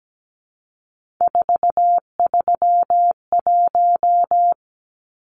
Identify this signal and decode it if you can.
Morse code, heard as a keyed tone, about 17 wpm, 700 Hz. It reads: 431
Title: Numbers - 3-Digit 17wpm